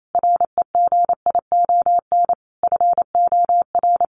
Morse code, heard as a keyed tone, about 28 wpm, 700 Hz. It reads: REGION FOR
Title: Sentences from Top 1500 Words 28wpm